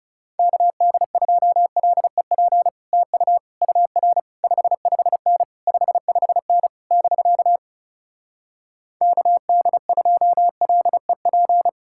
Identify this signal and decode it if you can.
Morse code, heard as a keyed tone, about 40 wpm, 700 Hz. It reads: KD2LEP TU UR 55N 55N <BK> KD2LEP